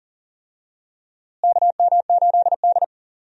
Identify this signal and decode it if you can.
Morse code, heard as a keyed tone, about 40 wpm, 700 Hz. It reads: KM8D
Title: USA Call Signs - 4 Characters - Repeated 2x - 40wpm